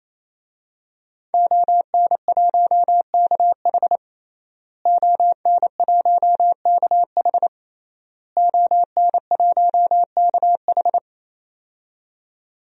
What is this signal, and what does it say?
Morse code, heard as a keyed tone, about 28 wpm, 700 Hz. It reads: ON1KH ON1KH ON1KH
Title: DXCC-100 Call Signs - 5 Characters - Repeated 3x - 28wpm